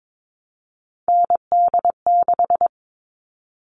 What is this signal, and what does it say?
Morse code, heard as a keyed tone, about 22 wpm, 700 Hz. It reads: ND6
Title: All Cut Numbers - 3-Digit 22wpm